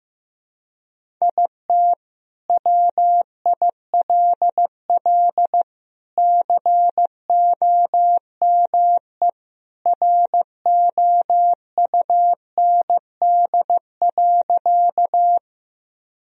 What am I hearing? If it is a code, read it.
Morse code, heard as a keyed tone, about 15 wpm, 700 Hz. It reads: IT WILL COME ROUND.
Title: Sentences from Top 1000 Words - Review Part 1 - 15wpm